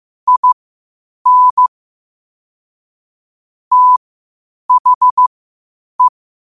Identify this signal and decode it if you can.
Morse code, heard as a keyed tone, about 15 wpm, spaced as for 8 wpm, 1000 Hz. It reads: IN THE
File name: qotdfiles / 8wpm